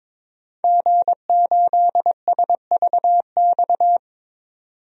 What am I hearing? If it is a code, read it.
Morse code, heard as a keyed tone, about 22 wpm, 700 Hz. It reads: G8SVX
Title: DXCC-100 Call Signs - 5 Characters - 22wpm